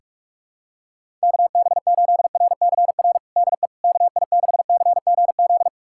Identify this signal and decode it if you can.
Morse code, heard as a keyed tone, about 41 wpm, 700 Hz. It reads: KB8RCR DE KI6CGZ
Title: USA Callsign DE Callsign - 6 Characters - 45wpm